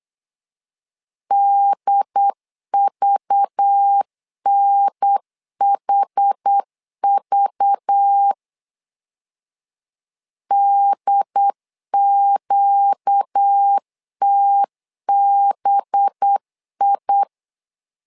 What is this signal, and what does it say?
Morse code, heard as a keyed tone, about 8 wpm, 800 Hz. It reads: DVNHV DQTBI
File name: LEZIONI / 5 LBVC / 5 LBV 8wpm 2837